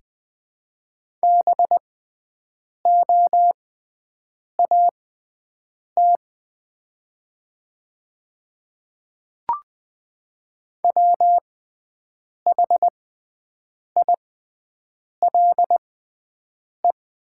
Morse code, heard as a keyed tone, about 20 wpm, spaced as for 7 wpm, 700 Hz. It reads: BOAT WHILE
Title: 6x Character Spacing - Top 500 Words - 20wpm